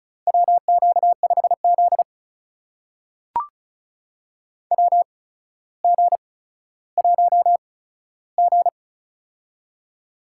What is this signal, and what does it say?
Morse code, heard as a keyed tone, about 36 wpm, spaced as for 15 wpm, 700 Hz. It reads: WQ5Z WG1G
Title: USA Call Signs - 4 Characters - 8x Character Spacing - 35wpm